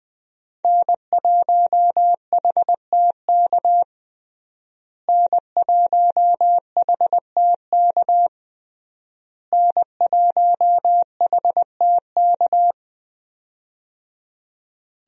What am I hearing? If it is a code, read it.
Morse code, heard as a keyed tone, about 20 wpm, 700 Hz. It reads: N1HTK N1HTK N1HTK
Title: USA Call Signs - 5 Characters - Repeated 3x - 20wpm